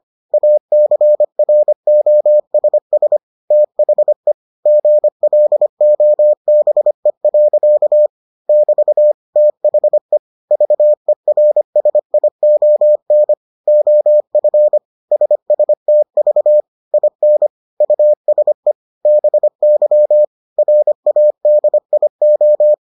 Morse code, heard as a keyed tone, about 25 wpm, 600 Hz. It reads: ACROSS THE GLOBE. <BT> THE VERSION OF SSTV IN USE BY RADIO